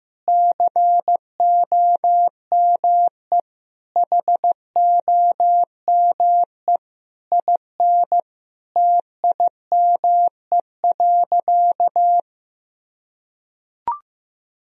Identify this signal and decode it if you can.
Morse code, heard as a keyed tone, about 15 wpm, 700 Hz. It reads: COME HOME IN TIME.
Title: Sentences from Top 1000 Words - Review Part 3 - 15wpm